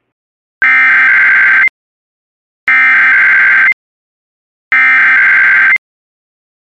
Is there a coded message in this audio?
Yes: Emergency Alert System (SAME) tones (header bursts) from 0.6 to 5.7 s, heard as harsh buzzing bursts.